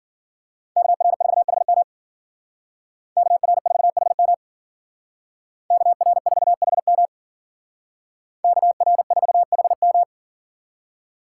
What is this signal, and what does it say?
Morse code, heard as a keyed tone, about 48 wpm, 700 Hz. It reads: KR4HM KR4HM KR4HM KR4HM
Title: Speed Racing - USA 2x2 Call Signs 40wpm